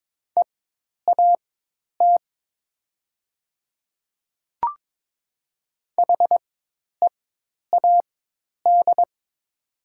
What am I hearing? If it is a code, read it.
Morse code, heard as a keyed tone, about 22 wpm, spaced as for 10 wpm, 700 Hz. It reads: EAT HEAD
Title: 4x Character Spacing - Top 300 Words - 22wpm